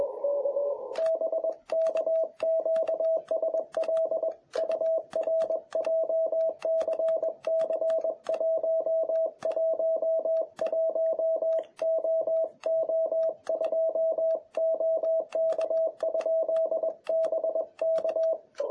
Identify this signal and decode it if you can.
Morse code, heard as a keyed tone, about 21 wpm, 650 Hz. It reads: BK QSL UR JCC111OO2 OK? BK
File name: jh8cbh / 2021cw